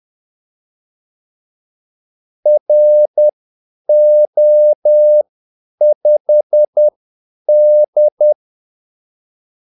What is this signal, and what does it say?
Morse code, heard as a keyed tone, about 10 wpm, spaced as for 8 wpm, 600 Hz. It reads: RO5D
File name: Callsigns-10-8wpm